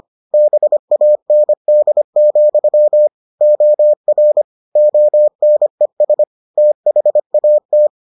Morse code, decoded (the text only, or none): BAND, OR ONES THAT